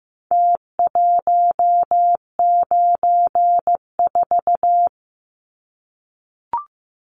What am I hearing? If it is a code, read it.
Morse code, heard as a keyed tone, about 15 wpm, 700 Hz. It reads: T194
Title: Numbers - Contest Style - 4-Digit 15wpm